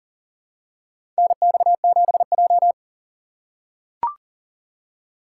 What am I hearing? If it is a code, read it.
Morse code, heard as a keyed tone, about 40 wpm, 700 Hz. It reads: NX7J
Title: USA Call Signs - 4 Characters - 6x Character Spacing - 40wpm